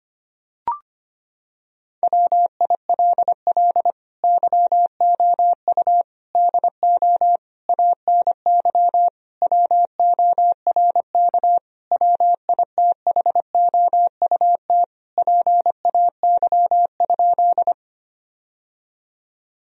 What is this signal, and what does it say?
Morse code, heard as a keyed tone, about 25 wpm, 700 Hz. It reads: WILL YOU DO ANY WORK WITHOUT PAY?
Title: Sentences from Top 1000 Words - Review Part 3 - 25wpm